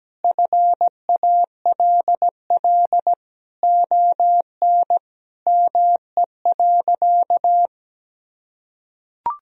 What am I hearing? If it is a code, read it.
Morse code, heard as a keyed tone, about 17 wpm, 700 Hz. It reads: FALL ON ME.
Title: Sentences from Top 1000 Words - Review Part 2 - 17wpm